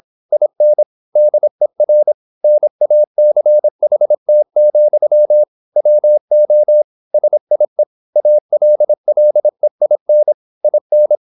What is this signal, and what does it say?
Morse code, heard as a keyed tone, about 26 wpm, 600 Hz. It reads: IN DER NACHT, WO SIE ALLEIN IN